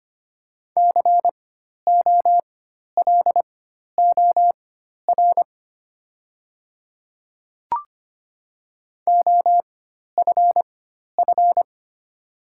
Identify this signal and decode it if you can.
Morse code, heard as a keyed tone, about 25 wpm, spaced as for 12 wpm, 690 Hz. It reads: COLOR OFF
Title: 4x Character Spacing - Top 300 Words - 25wpm